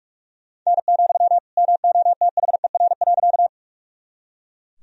Morse code, heard as a keyed tone, about 44 wpm, 700 Hz. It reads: N, MOTHER.
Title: Sentences from Top 200 Words and Abbreviations 45wpm